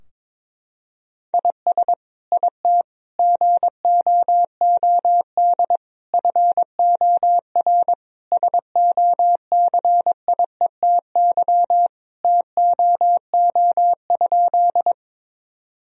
Morse code, heard as a keyed tone, about 22 wpm, 700 Hz. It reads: IS IT GOOD FOR SOCIETY TOO?
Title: Sentences from Top 1000 Words - Review Part 4 - 22wpm